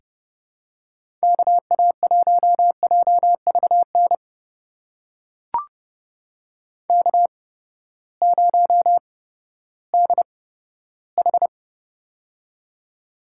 Morse code, heard as a keyed tone, about 30 wpm, spaced as for 15 wpm, 700 Hz. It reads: KA1JVN K0DH
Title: USA Call Signs - 8x Character Spacing - 30wpm